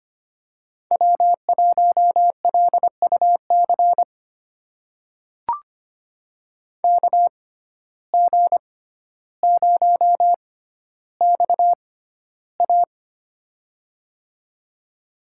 Morse code, heard as a keyed tone, about 25 wpm, spaced as for 13 wpm, 700 Hz. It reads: W1LUC KG0XA
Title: USA Call Signs - 5 Characters - 6x Character Spacing - 25wpm